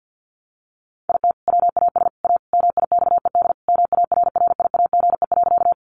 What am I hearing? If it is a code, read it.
Morse code, heard as a keyed tone, about 50 wpm, 700 Hz. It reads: IT WAS A MIXED MARRIAGE.
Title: Sentences from Top 1500 Words 50wpm